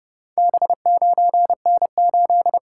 Morse code, heard as a keyed tone, about 32 wpm, 700 Hz. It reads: B9N8